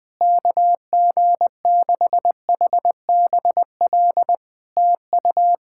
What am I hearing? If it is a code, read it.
Morse code, heard as a keyed tone, about 20 wpm, 700 Hz. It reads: KG6HBL TU
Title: Practice QSO - Exchange 4 - Ending - 20wpm